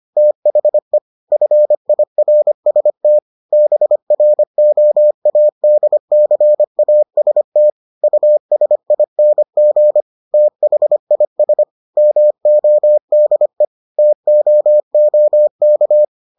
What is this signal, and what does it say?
Morse code, heard as a keyed tone, about 25 wpm, 600 Hz. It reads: THE FIRST BROADCAST USING THIS MODE TOOK